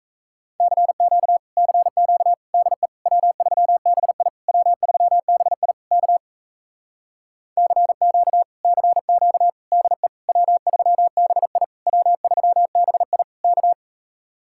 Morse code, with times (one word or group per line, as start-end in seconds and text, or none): 0.6-1.4 CQ
1.6-2.3 CQ
2.5-2.9 DE
3.1-4.3 W3BI
4.5-5.7 W3BI
5.9-6.2 K
7.6-8.4 CQ
8.6-9.5 CQ
9.7-10.1 DE
10.3-11.6 W3BI
11.9-13.2 W3BI
13.4-13.7 K